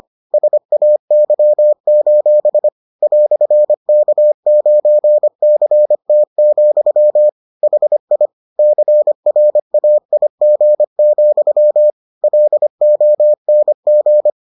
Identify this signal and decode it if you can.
Morse code, heard as a keyed tone, about 25 wpm, 600 Hz. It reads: SAY: "K9CT, HI CRAIG, LONG